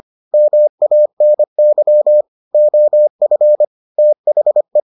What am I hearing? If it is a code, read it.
Morse code, heard as a keyed tone, about 25 wpm, 600 Hz. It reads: MANY OF THE